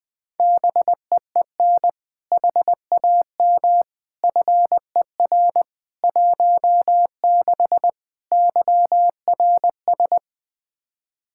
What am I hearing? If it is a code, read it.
Morse code, heard as a keyed tone, about 21 wpm, 700 Hz. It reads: BEEN HAM FER 16 YRS
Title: K1USN SST QSO Practice 20wpm